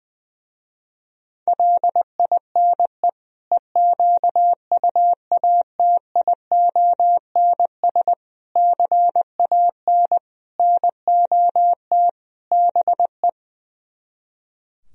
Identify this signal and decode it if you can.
Morse code, heard as a keyed tone, about 20 wpm, 700 Hz. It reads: LINE EQUATIONS CAN NOT BE
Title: Speed-Racing - Sets of 5 Words 20wpm